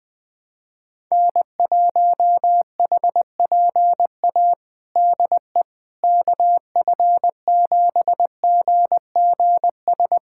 Morse code, heard as a keyed tone, about 20 wpm, 700 Hz. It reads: N1HPA DE KF7GGS
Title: USA Callsign DE Callsign 20wpm